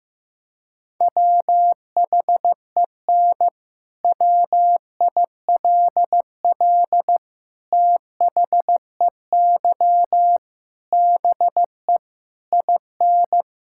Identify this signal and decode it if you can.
Morse code, heard as a keyed tone, about 15 wpm, 700 Hz. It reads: WHEN WILL THEY BE IN
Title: Sentences from Top 1000 Words - Review Part 6 - 15wpm